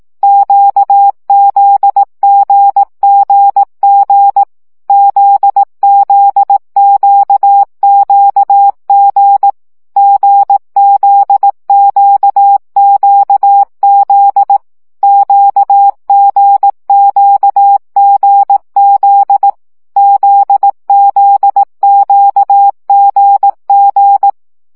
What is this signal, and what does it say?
Morse code, heard as a keyed tone, about 18 wpm, 800 Hz. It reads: QZGGG ZZQQG GZQQZ QGQGZ ZZQGG
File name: CW gyakorlas g z q 15 baud